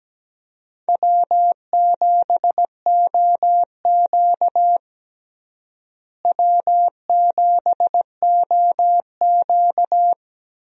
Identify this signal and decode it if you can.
Morse code, heard as a keyed tone, about 17 wpm, 700 Hz. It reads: W7OQ W7OQ
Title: USA Call Signs - 4 Characters - Repeated 3x - 17wpm